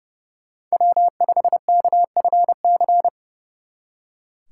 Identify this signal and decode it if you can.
Morse code, heard as a keyed tone, about 29 wpm, 700 Hz. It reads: W5KFC
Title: Speed Racing - USA 1x3 Call Signs 30wpm